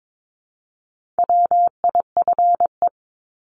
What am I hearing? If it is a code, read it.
Morse code, heard as a keyed tone, about 22 wpm, 700 Hz. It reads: WIFE
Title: Top 1751-2000 Words 22wpm